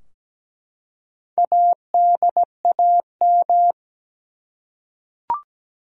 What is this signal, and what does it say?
Morse code, heard as a keyed tone, about 17 wpm, 700 Hz. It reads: ADAM
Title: USA Common Names - Max 5 Letters - 17wpm